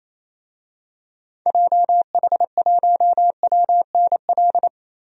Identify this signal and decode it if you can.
Morse code, heard as a keyed tone, about 28 wpm, 700 Hz. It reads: JH1WNL
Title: DXCC-100 Call Signs - 6 Characters - Repeated 3x - 28wpm